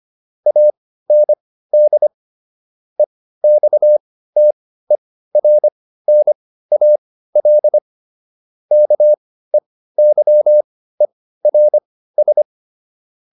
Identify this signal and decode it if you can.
Morse code, heard as a keyed tone, about 25 wpm, spaced as for 15 wpm, 600 Hz. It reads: AND EXTERNAL KEYERS